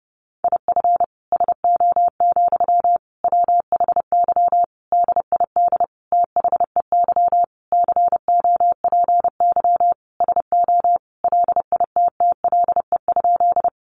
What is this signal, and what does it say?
Morse code, heard as a keyed tone, about 30 wpm, 700 Hz. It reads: IF SO, WHY DID THEY COPY SO LITTLE?
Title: Sentences from Top 1400 Words 30wpm